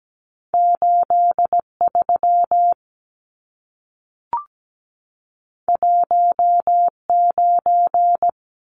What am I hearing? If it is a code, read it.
Morse code, heard as a keyed tone, about 17 wpm, 700 Hz. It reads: 83 19
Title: Numbers - 2-Digit 17wpm